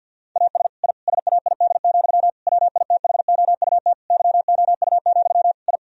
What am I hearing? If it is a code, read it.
Morse code, heard as a keyed tone, about 50 wpm, 700 Hz. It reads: AS I SAID, WITHOUT YOU, I
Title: Sentences from Top 1000 Words - Review Part 3 - 50wpm